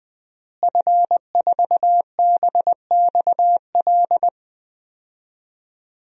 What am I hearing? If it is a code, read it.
Morse code, heard as a keyed tone, about 20 wpm, 700 Hz. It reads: F4BXL
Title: DXCC-100 Call Signs - 5 Characters - Repeated 3x - 20wpm